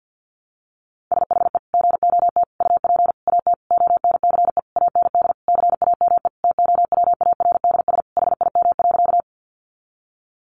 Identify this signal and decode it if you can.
Morse code, heard as a keyed tone, about 50 wpm, 700 Hz. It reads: SHE GOT UP AT ONCE AND CAME TOWARDS HIM.
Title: Sentences from Top 1100 Words 50wpm